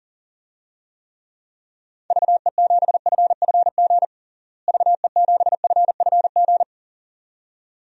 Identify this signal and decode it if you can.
Morse code, heard as a keyed tone, about 40 wpm, 700 Hz. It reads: VE7FFG VE7FFG
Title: DXCC-100 Call Signs - 6 Characters - Repeated 2x - 40wpm